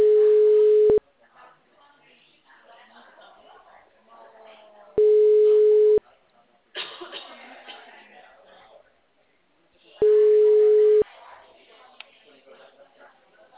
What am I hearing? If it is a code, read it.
Ringback tone (Europe, 425 Hz), with speech.
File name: out- -2014- - - .82547